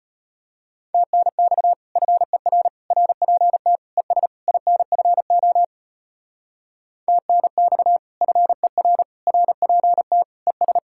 Morse code, heard as a keyed tone, about 32 wpm, 700 Hz. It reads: TNX FER RPT ES INFO TNX FER RPT ES